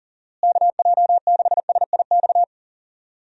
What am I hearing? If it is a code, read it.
Morse code, heard as a keyed tone, about 40 wpm, 700 Hz. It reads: KJ6SIX
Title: USA Call Signs - 6 Characters - Repeated 2x - 40wpm